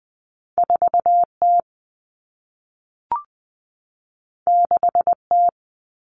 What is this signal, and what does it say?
Morse code, heard as a keyed tone, about 20 wpm, 700 Hz. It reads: 4T 6T
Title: All Cut Numbers - 2-Digit 20wpm